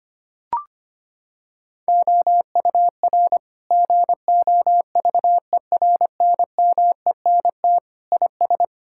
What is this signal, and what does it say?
Morse code, heard as a keyed tone, about 25 wpm, 700 Hz. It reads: OUR GOVERNMENT IS
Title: Sentences from Top 1000 Words - Review Part 5 - 25wpm